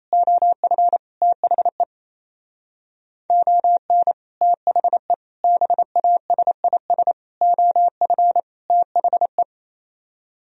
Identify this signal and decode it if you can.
Morse code, heard as a keyed tone, about 35 wpm, 700 Hz. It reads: OF THE ON THE BASIS OF THE
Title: Speed-Racing - Sets of 5 Words 28wpm